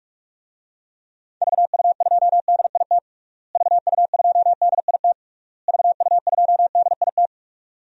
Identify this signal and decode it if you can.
Morse code, heard as a keyed tone, about 42 wpm, 700 Hz. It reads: VU2DIT VU2DIT VU2DIT
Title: DXCC-100 Call Signs - 6 Characters - Repeated 3x - 45wpm